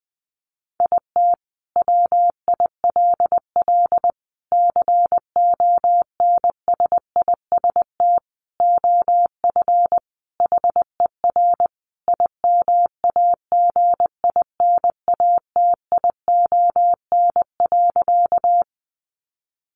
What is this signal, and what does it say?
Morse code, heard as a keyed tone, about 20 wpm, 700 Hz. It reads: IT WILL CONSIST OF HER IMAGINATION.
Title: Sentences from Top 1400 Words 20wpm